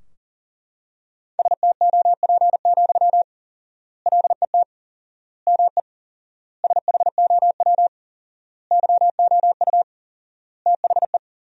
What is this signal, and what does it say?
Morse code, heard as a keyed tone, about 40 wpm, 700 Hz. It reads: STOP, LET ME SHOW YOU THE
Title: 4x Word-Spacing - Sentences from Top 300 Words - 40wpm